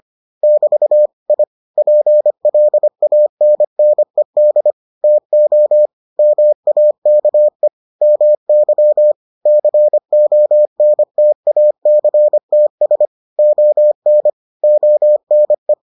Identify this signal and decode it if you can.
Morse code, heard as a keyed tone, about 25 wpm, 600 Hz. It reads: <BT> I PLANNED TO MAKE MY CONTACTS ON ONE